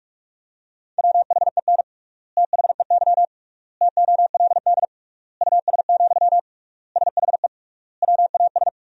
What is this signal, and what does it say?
Morse code, heard as a keyed tone, about 42 wpm, 700 Hz. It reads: WHEN THEY TOLD US, SHE WAS
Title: 3x Word-Spacing - Sentences from Top 400 Words - 45wpm